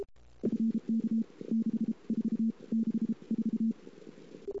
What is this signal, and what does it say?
Morse code, heard as a keyed tone, about 33 wpm, 220 Hz. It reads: FK 6 4 6 4
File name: CW / 2008-ss-cw-k5tr-op-n5rz -22:00